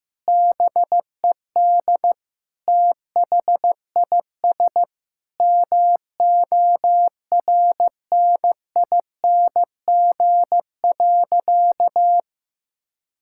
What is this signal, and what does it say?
Morse code, heard as a keyed tone, about 15 wpm, 700 Hz. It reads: BED THIS MORNING.
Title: Sentences from Top 1000 Words - Review Part 6 - 15wpm